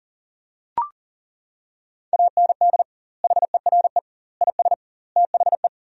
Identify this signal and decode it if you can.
Morse code, heard as a keyed tone, about 40 wpm, 700 Hz. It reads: AND HERE IS THE
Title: 2x Word-Spacing - Sentences from Top 500 Words - 40wpm